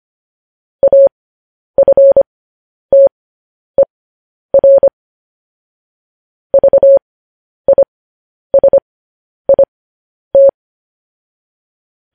Morse code, heard as a keyed tone, about 25 wpm, spaced as for 10 wpm, 550 Hz. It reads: AFTER VISIT
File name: podcast / News headlines-1x-25x10